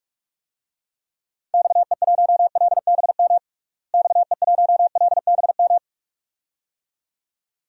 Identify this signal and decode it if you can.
Morse code, heard as a keyed tone, about 45 wpm, 700 Hz. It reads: XE1LBM XE1LBM
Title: DXCC-100 Call Signs - 6 Characters - Repeated 2x - 45wpm